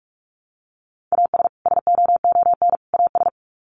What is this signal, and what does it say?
Morse code, heard as a keyed tone, about 44 wpm, 700 Hz. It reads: AS SOON AS
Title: Sets of 3 Words 45wpm